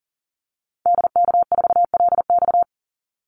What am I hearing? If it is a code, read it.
Morse code, heard as a keyed tone, about 40 wpm, 700 Hz. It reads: DK4LX
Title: Call Signs - 40wpm